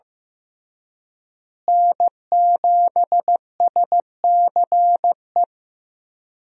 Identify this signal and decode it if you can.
Morse code, heard as a keyed tone, about 15 wpm, 700 Hz. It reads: N7SCE
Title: USA Call Signs - 5 Characters - Repeated 3x - 15wpm